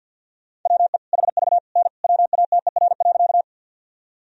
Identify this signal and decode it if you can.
Morse code, heard as a keyed tone, about 50 wpm, 700 Hz. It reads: WE HV N WATER.
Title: Sentences from Top 100 Words and Abbreviations 50wpm